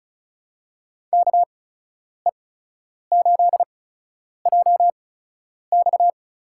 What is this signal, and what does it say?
Morse code, heard as a keyed tone, about 36 wpm, spaced as for 10 wpm, 700 Hz. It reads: KE8JX
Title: USA Call Signs - 8x Character Spacing - 35wpm